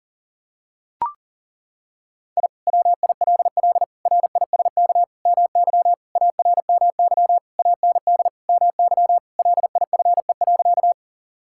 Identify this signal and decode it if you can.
Morse code, heard as a keyed tone, about 37 wpm, 700 Hz. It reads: I WILL RISK MY ARMY AND MY LIFE.
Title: Sentences from Top 1000 Words - Review Part 5 - 40wpm